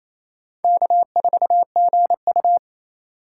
Morse code, heard as a keyed tone, about 28 wpm, 700 Hz. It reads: K4GU
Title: USA Call Signs - 6x Character Spacing - 28wpm